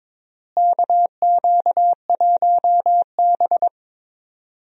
Morse code, heard as a keyed tone, about 22 wpm, 700 Hz. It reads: KQ1B
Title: USA Call Signs - 4 Characters - Repeated 2x - 22wpm